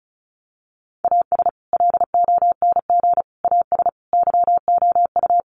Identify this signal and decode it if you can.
Morse code, heard as a keyed tone, about 34 wpm, 700 Hz. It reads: AS LONG AS YOU
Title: Sets of 4 Words 35wpm